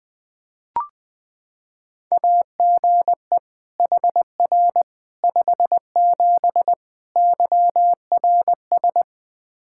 Morse code, heard as a keyed tone, about 20 wpm, 700 Hz. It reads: AGE HR 57 YRS
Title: K1USN SST QSO Practice 20wpm